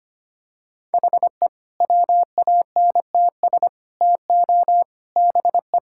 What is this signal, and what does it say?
Morse code, heard as a keyed tone, about 25 wpm, 700 Hz. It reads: HE WANTS TO BE
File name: practice / 25wpm / Sentences with 4-Word Phrases - 25wpm